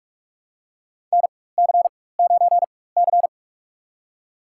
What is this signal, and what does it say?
Morse code, heard as a keyed tone, about 46 wpm, spaced as for 21 wpm, 700 Hz. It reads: NC9C
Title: USA Call Signs - 4x Character Spacing - 45wpm